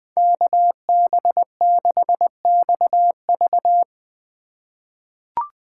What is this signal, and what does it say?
Morse code, heard as a keyed tone, about 20 wpm, 700 Hz. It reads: KB6XV
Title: USA Call Signs - 5 Characters - 4x Character Spacing - 20wpm